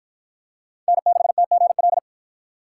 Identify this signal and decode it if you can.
Morse code, heard as a keyed tone, about 47 wpm, 700 Hz. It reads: N6TGL